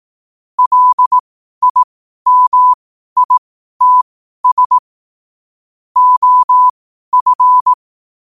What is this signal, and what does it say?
Morse code, heard as a keyed tone, about 18 wpm, spaced as for 12 wpm, 1000 Hz. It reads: LIMITS OF
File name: qotdfiles / 12wpm